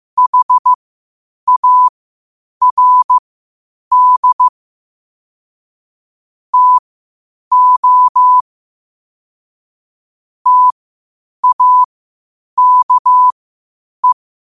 Morse code, heard as a keyed tone, about 15 wpm, spaced as for 8 wpm, 1000 Hz. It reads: HARD TO TAKE
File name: qotdfiles / 8wpm